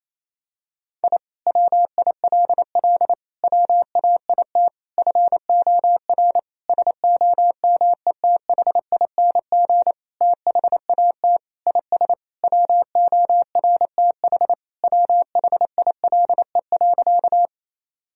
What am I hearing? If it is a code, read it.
Morse code, heard as a keyed tone, about 28 wpm, 700 Hz. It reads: I WILL WAIT FOR SOMETHING THAT IS WORTH WHILE.
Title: Sentences from Top 1000 Words - Review Part 1 - 28wpm